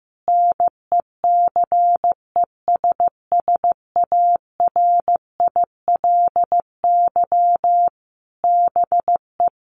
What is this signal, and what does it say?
Morse code, heard as a keyed tone, about 15 wpm, 700 Hz. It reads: NECESSARILY BE
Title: Sentences from Top 1400 Words 15wpm